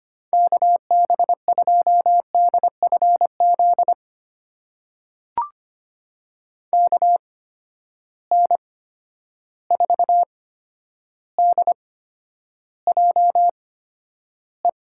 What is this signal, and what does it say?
Morse code, heard as a keyed tone, about 25 wpm, spaced as for 11 wpm, 700 Hz. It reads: KB2DFZ KN4DJE
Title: USA Call Signs - 6 Characters - 8x Character Spacing - 25wpm